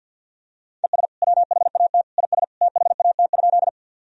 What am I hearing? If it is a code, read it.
Morse code, heard as a keyed tone, about 50 wpm, 700 Hz. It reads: ES WHAT IS THAT?
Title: Sentences from Top 100 Words and Abbreviations 50wpm